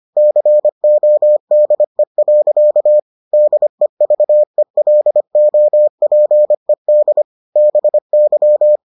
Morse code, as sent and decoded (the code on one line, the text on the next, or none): -.-. --- -.. .-.-.- / -.. ...- . .-.. --- .--. -.. / -... -.--
CODE. DEVELOPED BY